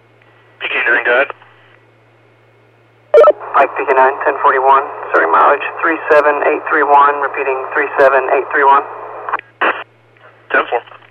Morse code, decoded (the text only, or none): none